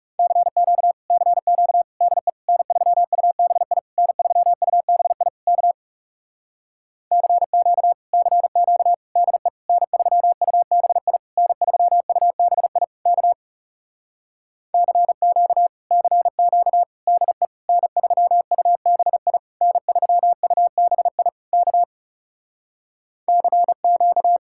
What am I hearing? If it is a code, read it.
Morse code, heard as a keyed tone, about 37 wpm, 700 Hz. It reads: CQ CQ DE N3UBI N3UBI K CQ CQ DE N3UBI N3UBI K CQ CQ DE N3UBI N3UBI K CQ
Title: Speed-Racing Common QSO Phrases Encore - 30wpm